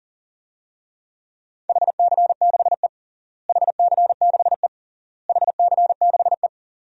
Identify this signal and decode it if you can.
Morse code, heard as a keyed tone, about 40 wpm, 700 Hz. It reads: HC6E HC6E HC6E
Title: DXCC-100 Call Signs - 4 Characters - Repeated 3x - 40wpm